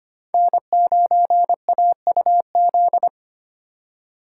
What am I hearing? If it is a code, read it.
Morse code, heard as a keyed tone, about 25 wpm, 700 Hz. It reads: N9AUZ
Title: USA Call Signs - 2x Character Spacing - 25wpm